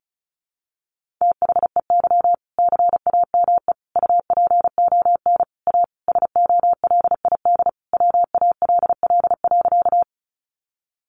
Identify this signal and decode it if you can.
Morse code, heard as a keyed tone, about 34 wpm, 700 Hz. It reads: THEY CAME UPON A SOLID WALL.
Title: Sentences from Top 1500 Words 35wpm